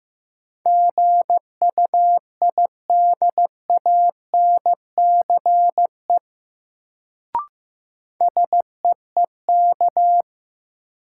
Morse code, heard as a keyed tone, about 15 wpm, 700 Hz. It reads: GUIDANCE SEEK
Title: Sentences from Top 1600 Words - New Word Drill Alt - 15wpm